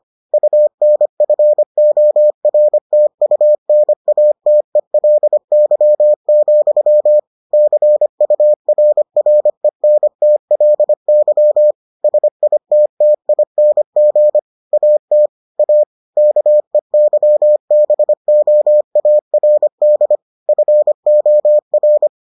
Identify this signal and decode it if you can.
Morse code, heard as a keyed tone, about 25 wpm, 600 Hz. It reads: UNFORTUNATELY, CURRENTLY SITTING AT A KEYBOARD FOR